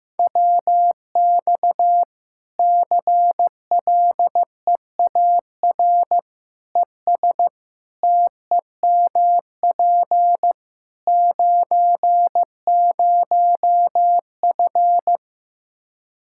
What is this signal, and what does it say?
Morse code, heard as a keyed tone, about 16 wpm, 700 Hz. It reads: WX CLEAR ES TEMP 90F
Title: Practice QSO - Exchange 2 - Info - 15wpm